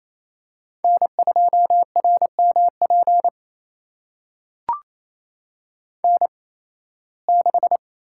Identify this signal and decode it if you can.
Morse code, heard as a keyed tone, about 28 wpm, spaced as for 18 wpm, 700 Hz. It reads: N2RMP N6
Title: USA Call Signs - 5 Characters - 8x Character Spacing - 28wpm